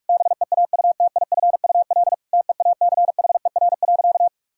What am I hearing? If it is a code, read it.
Morse code, heard as a keyed tone, about 46 wpm, 700 Hz. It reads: BEAUTIFUL TEACHER.
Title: Sentences from Top 1000 Words - Review Part 6 - 45wpm